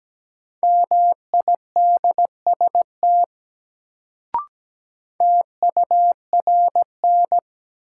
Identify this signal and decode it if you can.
Morse code, heard as a keyed tone, about 17 wpm, 700 Hz. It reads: MIDST TURN
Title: Sets of 4 Words - New Word Drill - 17wpm